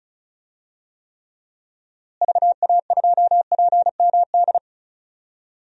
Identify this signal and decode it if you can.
Morse code, heard as a keyed tone, about 34 wpm, 700 Hz. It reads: VA2PMD
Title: DXCC-100 Call Signs - 6 Characters - Repeated 3x - 35wpm